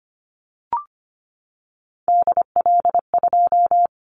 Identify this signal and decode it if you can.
Morse code, heard as a keyed tone, about 25 wpm, 700 Hz. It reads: DL2
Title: Callsign Prefixes for Top 25 DXCC - 25wpm